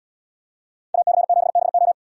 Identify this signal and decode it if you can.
Morse code, heard as a keyed tone, about 72 wpm, 700 Hz. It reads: AC7BO